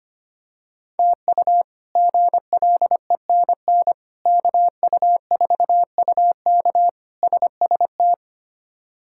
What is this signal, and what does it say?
Morse code, heard as a keyed tone, about 25 wpm, 700 Hz. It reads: TU GLENN KU4UK SST
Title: K1USN SST QSO Practice - 25wpm